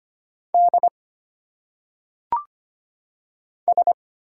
Morse code, heard as a keyed tone, about 25 wpm, 700 Hz. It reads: D S
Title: Single Letters - Phonetic Pronunciation 25wpm